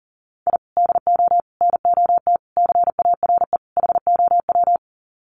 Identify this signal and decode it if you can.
Morse code, heard as a keyed tone, about 37 wpm, 700 Hz. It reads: I DO NOT CARE HOW
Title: Sets of 5 Words Encore2 40wpm